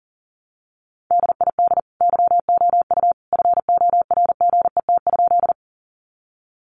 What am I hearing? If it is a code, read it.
Morse code, heard as a keyed tone, about 40 wpm, 700 Hz. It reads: DID YOU FORGET?
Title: Sentences from Top 1200 Words 40wpm